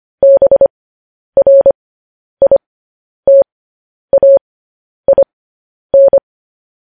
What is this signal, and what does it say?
Morse code, heard as a keyed tone, about 25 wpm, spaced as for 10 wpm, 550 Hz. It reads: BRITAIN
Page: news-headlines-in-morse-code-at-10-wpm